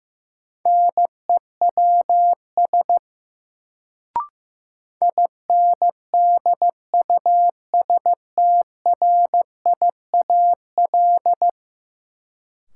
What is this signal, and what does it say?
Morse code, heard as a keyed tone, about 15 wpm, 700 Hz. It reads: NEWS INDUSTRIAL
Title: Sets of 5 Words - New Word Drill - 15wpm